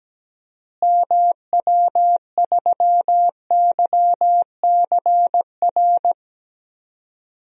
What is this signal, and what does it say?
Morse code, heard as a keyed tone, about 17 wpm, 700 Hz. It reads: MW3YCR
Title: DXCC-100 Call Signs - 6 Characters - Repeated 3x - 17wpm